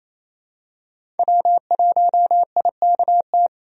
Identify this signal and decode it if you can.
Morse code, heard as a keyed tone, about 28 wpm, 700 Hz. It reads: W1IKT
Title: USA Call Signs - 5 Characters - Repeated 2x - 28wpm